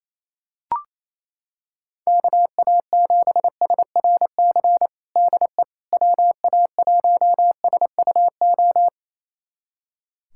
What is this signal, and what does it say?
Morse code, heard as a keyed tone, about 28 wpm, 700 Hz. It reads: KA7SRC DE WA1SUO
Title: USA Callsign DE Callsign - 6 Characters - 28wpm